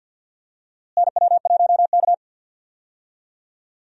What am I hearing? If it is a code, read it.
Morse code, heard as a keyed tone, about 52 wpm, 700 Hz. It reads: NW1K
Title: USA Call Signs - 4 Characters - Repeated 3x - 50wpm